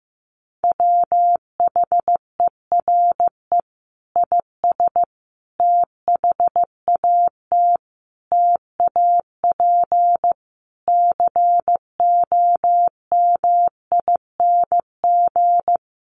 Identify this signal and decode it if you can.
Morse code, heard as a keyed tone, about 15 wpm, 700 Hz. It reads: WHERE IS THAT TAP COMING
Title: Sentences from Top 1400 Words 15wpm